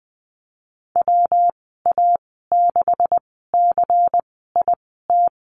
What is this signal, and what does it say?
Morse code, heard as a keyed tone, about 20 wpm, spaced as for 15 wpm, 700 Hz. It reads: WA6CIT
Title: 186 Lesson 30g - Letter b - Call Signs 20wpm